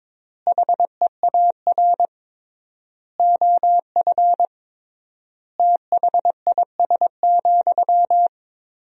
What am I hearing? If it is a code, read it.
Morse code, heard as a keyed tone, about 22 wpm, 700 Hz. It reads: HEAR OF THIS,
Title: 3x Word-Spacing - Sentences from Top 400 Words - 22wpm